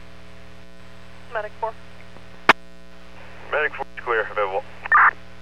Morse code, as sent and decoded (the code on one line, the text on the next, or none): none